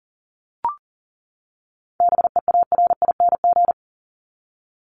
Morse code, heard as a keyed tone, about 40 wpm, 700 Hz. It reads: BEARING
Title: ING Suffix - Top 100 Words with Base Words 40wpm